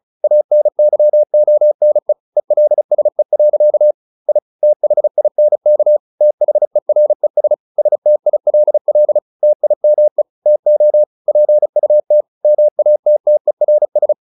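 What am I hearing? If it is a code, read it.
Morse code, heard as a keyed tone, about 34 wpm, 600 Hz. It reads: ANYONE ELSE. I THINK THERES STILL TIME TO PUT MATTERS
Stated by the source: Morse practice at 35 wpm from CW med Gnister, sending Ken Follett's novel Never in English.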